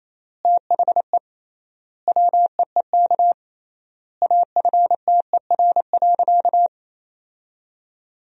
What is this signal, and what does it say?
Morse code, heard as a keyed tone, about 28 wpm, 700 Hz. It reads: THE WEEK AFTER.
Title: 3x Word-Spacing - Sentences from Top 500 Words - 28wpm